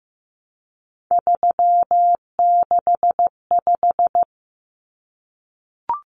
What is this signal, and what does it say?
Morse code, heard as a keyed tone, about 15 wpm, 700 Hz. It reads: 365
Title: Common Cut Numbers - 3-Digit 15wpm